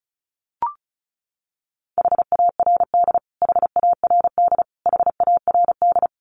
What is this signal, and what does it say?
Morse code, heard as a keyed tone, about 34 wpm, 700 Hz. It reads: HARD HARD HARD
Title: Top 500 Words Repeated 35wpm